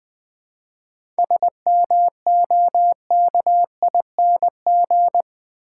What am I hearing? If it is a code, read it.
Morse code, heard as a keyed tone, about 20 wpm, 700 Hz. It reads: SMOKING
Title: 7 Letter Words - First 3 Letters Given - 20wpm